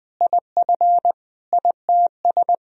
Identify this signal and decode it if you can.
Morse code, heard as a keyed tone, about 20 wpm, 700 Hz. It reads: IF ITS
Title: Sentences from Top 1000 Words - Review Part 3 - 20wpm